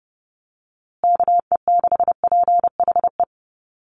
Morse code, heard as a keyed tone, about 30 wpm, 700 Hz. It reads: KE6PHE
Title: USA Call Signs Repeated 3x 30wpm